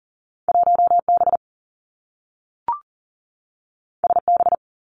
Morse code, heard as a keyed tone, about 38 wpm, 700 Hz. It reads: 1 B S B
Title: Two Challenging Letter-Number Combinations 40wpm